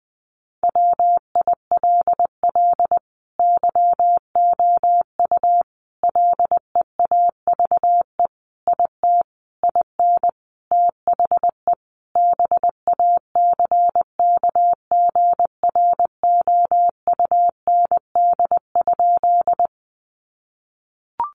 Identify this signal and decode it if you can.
Morse code, heard as a keyed tone, about 20 wpm, 700 Hz. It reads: WILL YOU LEAVE IT IN THE BACKGROUND?
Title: Sentences from Top 1300 Words 20wpm